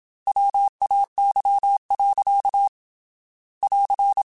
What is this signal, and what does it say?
Morse code, heard as a keyed tone, about 27 wpm, 800 Hz. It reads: WAY. <AR>
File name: Technical Subjects / Technical Subjects-mp3s / Gates Intro @25